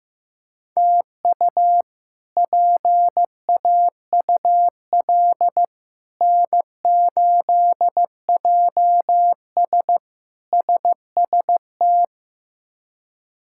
Morse code, heard as a keyed tone, about 15 wpm, 700 Hz. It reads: TU PAUL N8JS SST
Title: K1USN SST QSO Practice - 15wpm